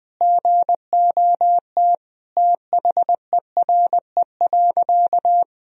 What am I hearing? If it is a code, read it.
Morse code, heard as a keyed tone, about 20 wpm, 700 Hz. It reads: GOT THERE.
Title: Sentences from Top 1000 Words - Review Part 1 - 20wpm